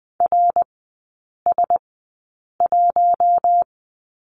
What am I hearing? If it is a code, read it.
Morse code, heard as a keyed tone, about 20 wpm, spaced as for 8 wpm, 700 Hz. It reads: RS1
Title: 100 Lesson 17e - Letter u - Three Characters 20wpm